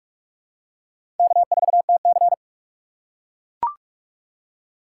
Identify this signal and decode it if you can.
Morse code, heard as a keyed tone, about 41 wpm, 700 Hz. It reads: K4TC
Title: USA Call Signs - 4 Characters - 4x Character Spacing - 45wpm